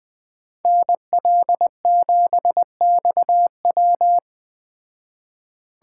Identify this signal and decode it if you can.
Morse code, heard as a keyed tone, about 20 wpm, 700 Hz. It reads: NL7XW
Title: Speed Racing - USA 2x2 Call Signs 17wpm